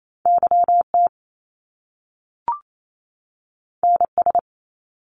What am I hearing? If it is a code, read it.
Morse code, heard as a keyed tone, about 28 wpm, 700 Hz. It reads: YT NS
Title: Canadian Provinces and Territories 28wpm